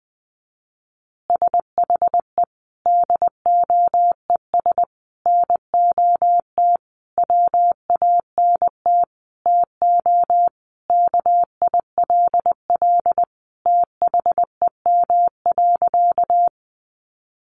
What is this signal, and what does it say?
Morse code, heard as a keyed tone, about 20 wpm, 700 Hz. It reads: SHE DOES NOT WANT TO KILL THEM.
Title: Easy Phrases 20wpm